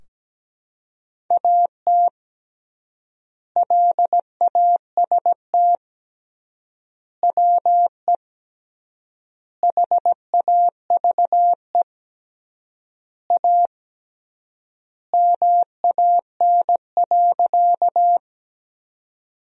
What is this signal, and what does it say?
Morse code, heard as a keyed tone, about 17 wpm, 700 Hz. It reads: AT LAST WE HAVE A MAN.
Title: 3x Word-Spacing - Sentences from Top 300 Words - 17wpm